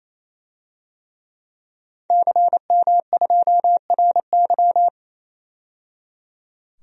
Morse code, heard as a keyed tone, about 28 wpm, 700 Hz. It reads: CM2RY
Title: DXCC-100 Call Signs - 5 Characters - 28wpm